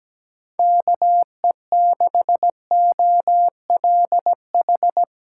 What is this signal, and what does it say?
Morse code, heard as a keyed tone, about 17 wpm, 700 Hz. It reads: KE6OLH
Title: USA Callsign DE Callsign - 6 Characters - 17wpm